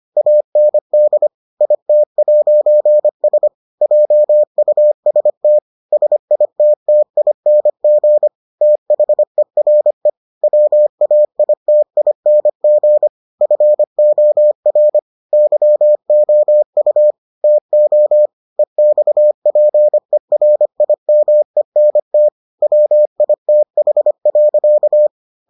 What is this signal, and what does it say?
Morse code, heard as a keyed tone, about 25 wpm, 600 Hz. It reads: AND IT'S JUST SITTING THERE WAITING FOR YOU TO EXPERIMENT WITH.